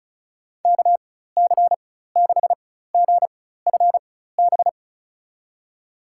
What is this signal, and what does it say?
Morse code, heard as a keyed tone, about 36 wpm, spaced as for 16 wpm, 700 Hz. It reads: KC6GFB
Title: USA Call Signs - 4x Character Spacing - 35wpm